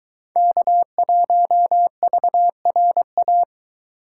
Morse code, heard as a keyed tone, about 22 wpm, 700 Hz. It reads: K1VRA